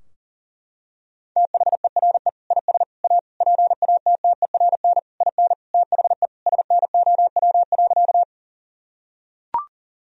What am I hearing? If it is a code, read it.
Morse code, heard as a keyed tone, about 40 wpm, 700 Hz. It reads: THERE IS A PATTERN IN THE SNOW.
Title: Sentences from Top 1000 Words - Review Part 2 - 40wpm